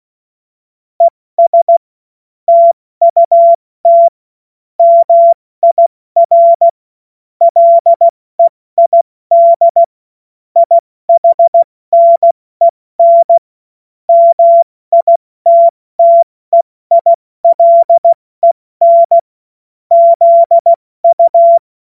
Morse code, heard as a keyed tone, about 16 wpm, 680 Hz. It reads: ES TUT MIR LEID IHNEN MITTEILEN ZU
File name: Morsekurs / Witze / Witz 27. Juni 2023 70 F80